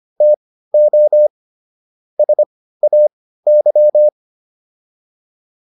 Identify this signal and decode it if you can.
Morse code, heard as a keyed tone, about 25 wpm, spaced as for 15 wpm, 600 Hz. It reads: TO SAY